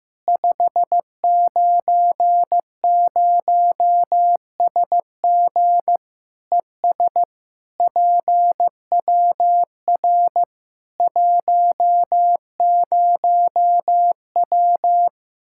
Morse code, heard as a keyed tone, about 15 wpm, 700 Hz. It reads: 590SG ES PWR 10W
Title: Practice QSO - Exchange 2 - Info - 15wpm